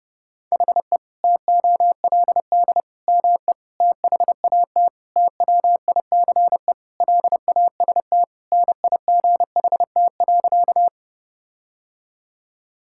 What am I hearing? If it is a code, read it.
Morse code, heard as a keyed tone, about 32 wpm, 700 Hz. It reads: HE TOLD ME THAT TWICE LAST NIGHT.
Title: Sentences from Top 1600 Words 30wpm